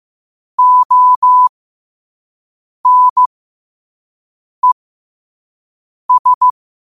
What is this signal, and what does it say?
Morse code, heard as a keyed tone, about 15 wpm, spaced as for 5 wpm, 1000 Hz. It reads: ONES
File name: qotdfiles / 5wpm